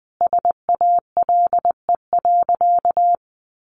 Morse code, heard as a keyed tone, about 20 wpm, 700 Hz. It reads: SALE.
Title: Sentences from Top 1400 Words 20wpm